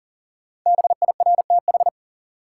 Morse code, heard as a keyed tone, about 40 wpm, 700 Hz. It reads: BIRTH
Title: Sentences from Top 800 Words - New Word Drill - 40wpm